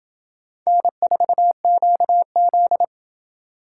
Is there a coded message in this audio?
Yes: Morse code from 0.7 to 2.8 s, heard as a keyed tone, about 28 wpm, 700 Hz. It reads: N4QZ